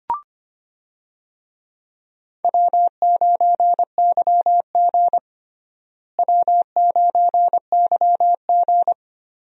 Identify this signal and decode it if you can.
Morse code, heard as a keyed tone, about 25 wpm, 700 Hz. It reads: W9YG W9YG
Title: USA Call Signs - 4 Characters - Repeated 2x - 25wpm